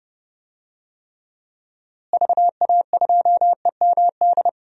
Morse code, heard as a keyed tone, about 30 wpm, 700 Hz. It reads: VA2EMD
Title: DXCC-100 Call Signs - 6 Characters - Repeated 3x - 30wpm